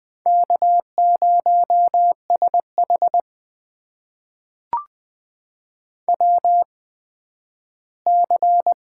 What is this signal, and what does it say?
Morse code, heard as a keyed tone, about 20 wpm, spaced as for 12 wpm, 700 Hz. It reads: K0SH WC
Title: USA Call Signs - 4 Characters - 8x Character Spacing - 20wpm